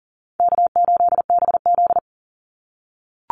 Morse code, heard as a keyed tone, about 40 wpm, 700 Hz. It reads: K8BZ
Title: Easy Call Signs - 40wpm